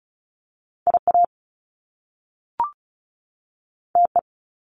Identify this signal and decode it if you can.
Morse code, heard as a keyed tone, about 36 wpm, 700 Hz. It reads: IA TE
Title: Top Two Letter Combinations in Words 35wpm